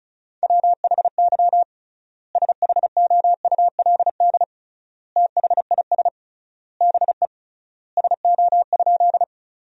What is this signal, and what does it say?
Morse code, heard as a keyed tone, about 36 wpm, 700 Hz. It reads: WHY SHOULD THIS BE SO?
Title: 3x Word-Spacing - Sentences from Top 300 Words - 35wpm